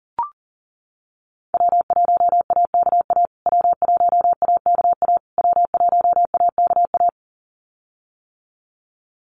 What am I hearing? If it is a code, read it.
Morse code, heard as a keyed tone, about 40 wpm, 700 Hz. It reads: W1AKA W1AKA W1AKA
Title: USA Call Signs Repeated 3x 40wpm